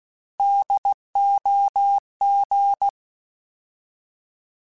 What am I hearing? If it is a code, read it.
Morse code, heard as a keyed tone, about 16 wpm, 800 Hz. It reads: DOG
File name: Alphabetical / Alphabetical mp3s / Ds @15